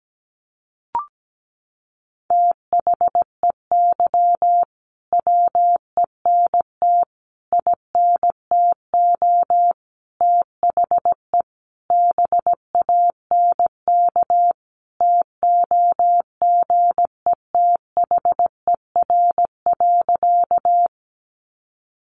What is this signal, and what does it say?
Morse code, heard as a keyed tone, about 17 wpm, 700 Hz. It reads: THEY WENT INTO THE BANK TOGETHER.
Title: Sentences from Top 900 Words 17wpm